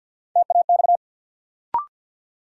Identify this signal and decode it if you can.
Morse code, heard as a keyed tone, about 49 wpm, 700 Hz. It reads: TAX
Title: Sentences from Top 1000 Words - Review Part 5 - New Word Drill - 50wpm